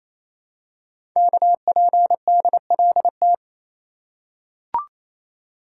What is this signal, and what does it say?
Morse code, heard as a keyed tone, about 27 wpm, 700 Hz. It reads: KPDLT
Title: Code Groups - 5 Letters - 28wpm